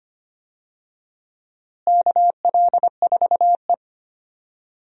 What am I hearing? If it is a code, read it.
Morse code, heard as a keyed tone, about 25 wpm, 700 Hz. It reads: KL4E
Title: USA Call Signs - 4 Characters - Repeated 2x - 25wpm